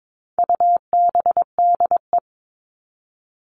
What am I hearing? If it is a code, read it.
Morse code, heard as a keyed tone, about 22 wpm, 700 Hz. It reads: UBDE
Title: All Cut Numbers - 4-Digit 22wpm